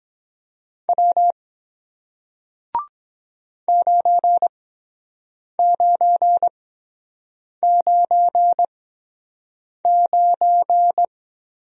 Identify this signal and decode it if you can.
Morse code, heard as a keyed tone, about 23 wpm, spaced as for 7 wpm, 700 Hz. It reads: W 9999